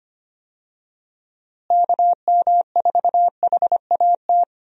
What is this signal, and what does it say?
Morse code, heard as a keyed tone, about 25 wpm, 700 Hz. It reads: KM4HAT
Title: USA Call Signs - 6 Characters - Repeated 2x - 25wpm